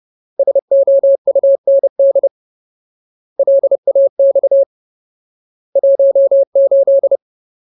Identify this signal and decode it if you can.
Morse code, heard as a keyed tone, about 30 wpm, 550 Hz. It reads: SOUND LAX 18
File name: Ultimate Word List / 30WPM Ultimate